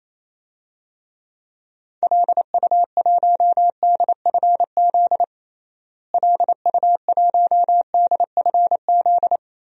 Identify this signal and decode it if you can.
Morse code, heard as a keyed tone, about 28 wpm, 700 Hz. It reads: LU1DFZ LU1DFZ
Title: DXCC-100 Call Signs - 6 Characters - Repeated 2x - 28wpm